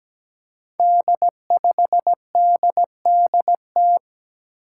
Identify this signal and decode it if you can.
Morse code, heard as a keyed tone, about 17 wpm, 700 Hz. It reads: D5DDT
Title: Five Challenging Letter-Number Combinations 17wpm